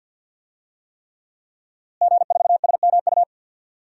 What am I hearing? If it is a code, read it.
Morse code, heard as a keyed tone, about 52 wpm, 700 Hz. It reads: G4SMU
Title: DXCC-100 Call Signs - 5 Characters - Repeated 3x - 50wpm